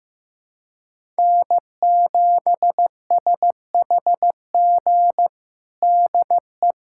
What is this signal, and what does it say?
Morse code, heard as a keyed tone, about 15 wpm, 700 Hz. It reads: N7SHG DE
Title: USA Callsign DE Callsign - 5 Characters - 15wpm